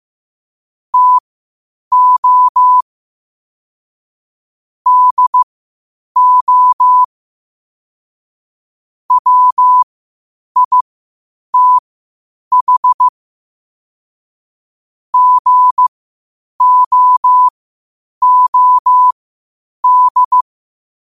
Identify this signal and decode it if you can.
Morse code, heard as a keyed tone, about 15 wpm, spaced as for 8 wpm, 1000 Hz. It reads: TO DO WITH GOOD
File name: qotdfiles / 8wpm